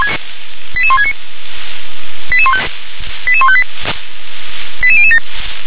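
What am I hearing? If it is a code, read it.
Five-tone selective call (ZVEI), heard as quick runs of five tones, tones of about 70 ms. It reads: 88169 88158 88007
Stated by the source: Five-tone ZVEI selective call received by SDR.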